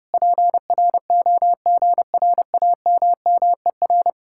Morse code, heard as a keyed tone, about 33 wpm, 700 Hz. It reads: PROGRAMMER